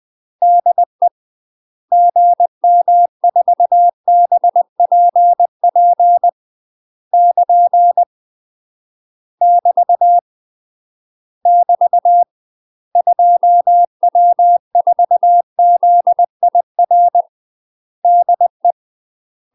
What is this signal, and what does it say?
Morse code, heard as a keyed tone, about 20 wpm, 700 Hz. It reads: DE GM4BPP <KN> <BT> <BT> 2W4ZIR DE
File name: UK MASTER QSOs TRAINING / 20WPM / UK MASTER QSOs 0017